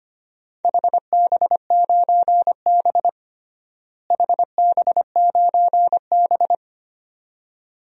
Morse code, heard as a keyed tone, about 25 wpm, 700 Hz. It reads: HB9B HB9B
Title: DXCC-100 Call Signs - 4 Characters - Repeated 2x - 25wpm